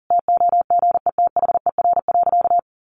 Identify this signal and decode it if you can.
Morse code, heard as a keyed tone, about 40 wpm, 700 Hz. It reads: TOGETHER.
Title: Sentences from Top 500 Words 40wpm